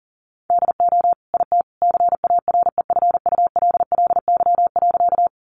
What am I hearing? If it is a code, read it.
Morse code, heard as a keyed tone, about 40 wpm, 700 Hz. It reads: DO IT CAREFULLY.
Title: Sentences from Top 1300 Words 40wpm